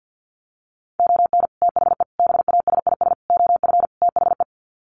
Morse code, heard as a keyed tone, about 50 wpm, 700 Hz. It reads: ON THE BASIS OF THE
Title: Sets of 5 Words 50wpm